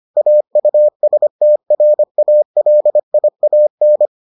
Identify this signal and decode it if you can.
Morse code, heard as a keyed tone, about 25 wpm, 600 Hz. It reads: AUSTRALIAN